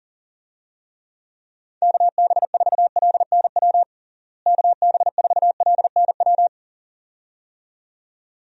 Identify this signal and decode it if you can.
Morse code, heard as a keyed tone, about 40 wpm, 700 Hz. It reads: KB4LNW KB4LNW
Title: USA Call Signs - 6 Characters - Repeated 2x - 40wpm